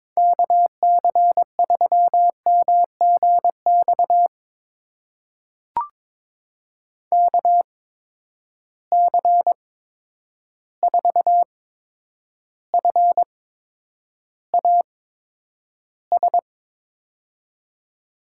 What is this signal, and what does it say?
Morse code, heard as a keyed tone, about 22 wpm, spaced as for 9 wpm, 700 Hz. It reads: KC3MGX KC4FAS
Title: USA Call Signs - 6 Characters - 8x Character Spacing - 22wpm